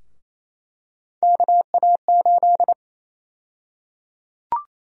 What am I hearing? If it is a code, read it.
Morse code, heard as a keyed tone, about 28 wpm, 700 Hz. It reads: KA8
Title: USA Call Sign Prefixes - Repeated 2x - 28wpm